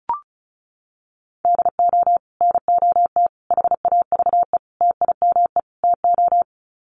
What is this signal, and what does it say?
Morse code, heard as a keyed tone, about 34 wpm, 700 Hz. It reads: DO NOT HAVE TIME TO
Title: Sets of 5 Words 35wpm